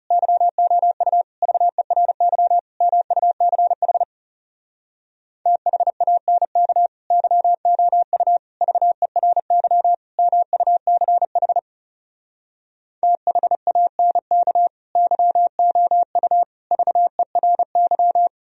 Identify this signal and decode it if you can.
Morse code, heard as a keyed tone, about 40 wpm, 700 Hz. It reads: YOU VERY MUCH THANK YOU VERY MUCH THANK YOU VERY